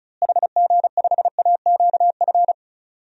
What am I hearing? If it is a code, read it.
Morse code, heard as a keyed tone, about 36 wpm, 700 Hz. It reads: HG5AQF
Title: DXCC-100 Call Signs - 6 Characters - Repeated 3x - 35wpm